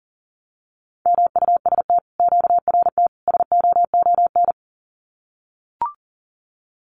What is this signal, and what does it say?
Morse code, heard as a keyed tone, about 41 wpm, 700 Hz. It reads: MUST QRT SOON
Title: Common QSO Phrases Encore 40wpm